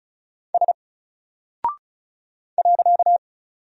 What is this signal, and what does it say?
Morse code, heard as a keyed tone, about 35 wpm, 700 Hz. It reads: S .
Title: Single Character (Letter:Number:Punctuation:Prosign) 35wpm